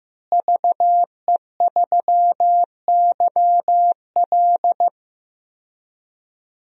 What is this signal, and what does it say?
Morse code, heard as a keyed tone, about 15 wpm, 700 Hz. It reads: VE3YL
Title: DXCC-100 Call Signs - 5 Characters - Repeated 3x - 15wpm